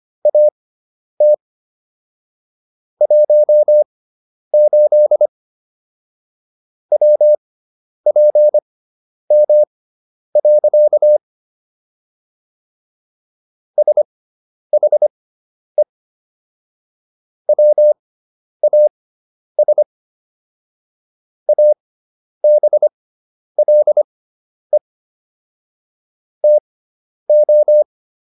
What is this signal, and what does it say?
Morse code, heard as a keyed tone, about 25 wpm, spaced as for 10 wpm, 600 Hz. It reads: AT 18 WPM. SHE WAS ABLE TO